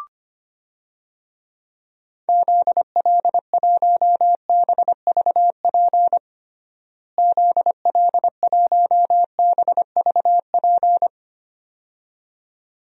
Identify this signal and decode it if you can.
Morse code, heard as a keyed tone, about 25 wpm, 700 Hz. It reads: ZL1BVP ZL1BVP
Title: DXCC-100 Call Signs - 6 Characters - Repeated 2x - 25wpm